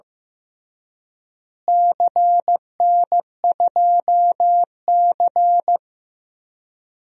Morse code, heard as a keyed tone, about 15 wpm, 700 Hz. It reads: CN2C
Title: DXCC-100 Call Signs - 4 Characters - Repeated 3x - 15wpm